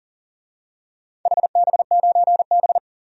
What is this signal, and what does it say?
Morse code, heard as a keyed tone, about 40 wpm, 700 Hz. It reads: HB9B
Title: DXCC-100 Call Signs - 4 Characters - 40wpm